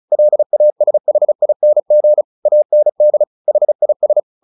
Morse code, heard as a keyed tone, about 34 wpm, 600 Hz. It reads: LASHING AND HIS
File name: the old man and the sea 2 35wpm